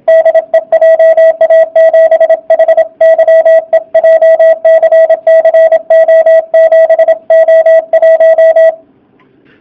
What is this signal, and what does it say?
Morse code, heard as a keyed tone, about 27 wpm, 660 Hz, with radio noise. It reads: DE JA7HYE JCC O7O1
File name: jh8cbh / 2019cw / sp32- w